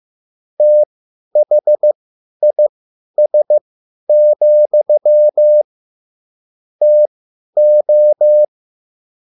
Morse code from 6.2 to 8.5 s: TO